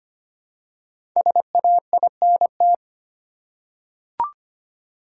Morse code, heard as a keyed tone, about 25 wpm, 700 Hz. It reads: SAINT
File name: 5 Letter Words - First 3 Letters Given - 25wpm